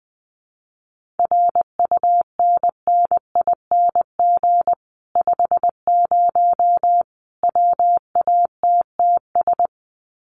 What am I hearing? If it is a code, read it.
Morse code, heard as a keyed tone, about 20 wpm, 700 Hz. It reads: RUNNING 50 WATTS
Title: Common QSO Phrases 20wpm